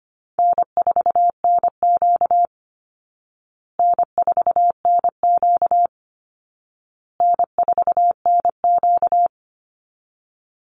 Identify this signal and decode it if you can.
Morse code, heard as a keyed tone, about 25 wpm, 700 Hz. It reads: N4NQ N4NQ N4NQ
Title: Call Signs Repeated 2x - 25wpm